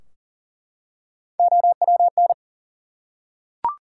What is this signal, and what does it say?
Morse code, heard as a keyed tone, about 38 wpm, 700 Hz. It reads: OWN
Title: Three Letter Words - 40wpm